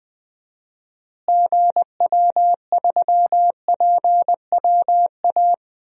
Morse code, heard as a keyed tone, about 20 wpm, 700 Hz. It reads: GW3PWA
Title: DXCC-100 Call Signs - 6 Characters - Repeated 3x - 20wpm